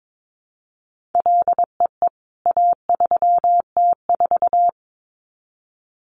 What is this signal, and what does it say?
Morse code, heard as a keyed tone, about 22 wpm, 700 Hz. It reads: LEE A3T4
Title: CWT Exchange 22wpm